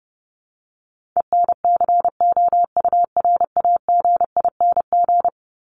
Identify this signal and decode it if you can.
Morse code, heard as a keyed tone, about 30 wpm, 700 Hz. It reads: ENCOURAGING
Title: Top 1751-2000 Words 30wpm